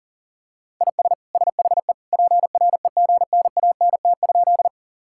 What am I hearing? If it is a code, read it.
Morse code, heard as a keyed tone, about 40 wpm, 700 Hz. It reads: IS SHE PREGNANT?
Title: Sentences from Top 1000 Words - Review Part 6 - 40wpm